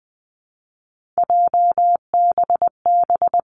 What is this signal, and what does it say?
Morse code, heard as a keyed tone, about 20 wpm, 700 Hz. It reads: JBB
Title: Three Challenging Letter-Number Combinations 20wpm